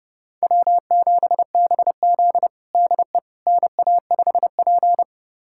Morse code, heard as a keyed tone, about 32 wpm, 700 Hz. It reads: W7BZ DE NA5P
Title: USA Callsign DE Callsign - 4 Characters - 30wpm